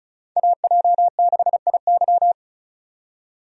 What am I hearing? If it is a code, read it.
Morse code, heard as a keyed tone, about 36 wpm, 700 Hz. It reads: AJ6IY
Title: USA Call Signs - 5 Characters - 35wpm